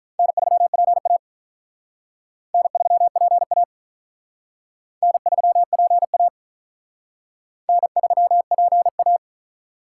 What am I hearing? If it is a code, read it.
Morse code, heard as a keyed tone, about 45 wpm, 700 Hz. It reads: N3PA N3PA N3PA N3PA